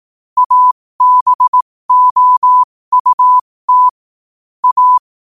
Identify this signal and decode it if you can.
Morse code, heard as a keyed tone, about 18 wpm, spaced as for 15 wpm, 1000 Hz. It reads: ABOUT A
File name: qotdfiles / 15wpm